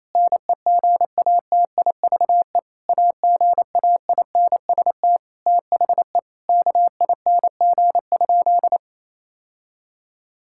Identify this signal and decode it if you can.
Morse code, heard as a keyed tone, about 27 wpm, 700 Hz. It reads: NEGATIVE AGAINST THE KING?
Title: Sentences from Top 1000 Words - Review Part 6 - 28wpm